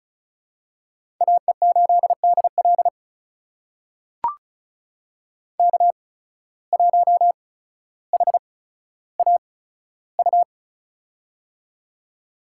Morse code, heard as a keyed tone, about 36 wpm, spaced as for 15 wpm, 700 Hz. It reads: AE8DL K1HAU
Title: USA Call Signs - 8x Character Spacing - 35wpm